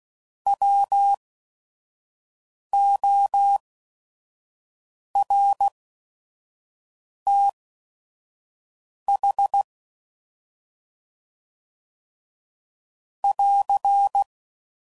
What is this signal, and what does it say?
Morse code, heard as a keyed tone, about 16 wpm, spaced as for 5 wpm, 800 Hz. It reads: WORTH <AR>
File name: Prefixes / By First Letter / Ws